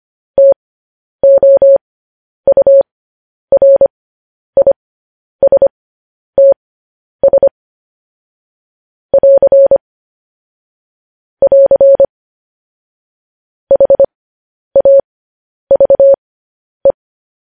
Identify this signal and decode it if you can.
Morse code, heard as a keyed tone, about 25 wpm, spaced as for 10 wpm, 550 Hz. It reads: TOURISTS <AR> <AR> HAVE
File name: podcast / News headlines-1x-25x10